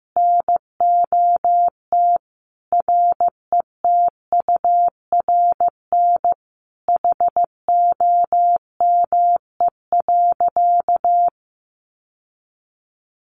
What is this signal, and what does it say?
Morse code, heard as a keyed tone, about 15 wpm, 700 Hz. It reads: NOT RETURN HOME.
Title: Sentences from Top 800 Words 15wpm